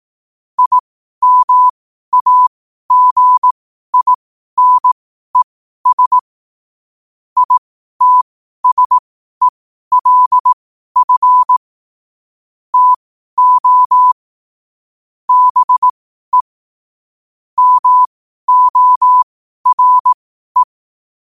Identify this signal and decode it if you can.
Morse code, heard as a keyed tone, about 18 wpm, spaced as for 12 wpm, 1000 Hz. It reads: IMAGINES ITSELF TO BE MORE
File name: qotdfiles / 12wpm